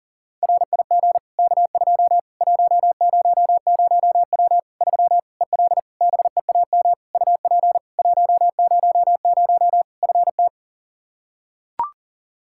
Morse code, heard as a keyed tone, about 37 wpm, 700 Hz. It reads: RIG K2 100W 3 EL BEAM UP 100 FT
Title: K1USN SST QSO Practice 40wpm